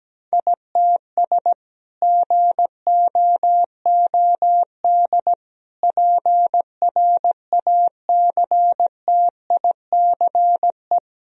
Morse code, heard as a keyed tone, about 17 wpm, 700 Hz. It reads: ITS GOOD PRACTICE
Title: Sentences from Top 1000 Words - Review Part 4 - 17wpm